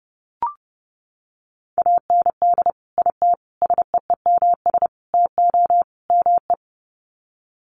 Morse code, heard as a keyed tone, about 30 wpm, 700 Hz. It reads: AND IT SEEMS TO ME
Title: Sets of 5 Words Encore2 30wpm